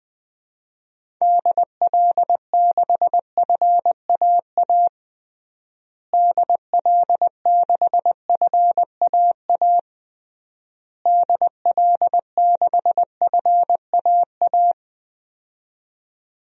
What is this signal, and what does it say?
Morse code, heard as a keyed tone, about 19 wpm, 700 Hz. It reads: DL6FAA DL6FAA DL6FAA
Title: DXCC-100 Call Signs - 6 Characters - Repeated 3x - 20wpm